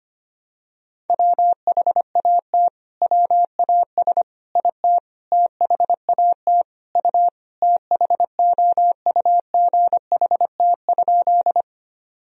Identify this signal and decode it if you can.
Morse code, heard as a keyed tone, about 25 wpm, 700 Hz. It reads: WHAT WAS IT THAT U THOUGHT?
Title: Sentences from Top 300 Words and Abbreviations 25wpm